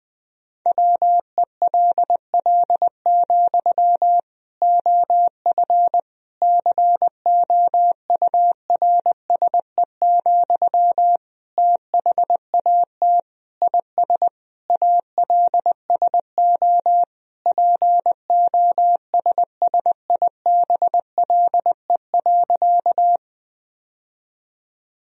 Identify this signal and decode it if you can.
Morse code, heard as a keyed tone, about 20 wpm, 700 Hz. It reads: WELL, OF COURSE, THAT IS ALSO POSSIBLE.
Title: Sentences from Top 1000 Words - Review Part 1 - 20wpm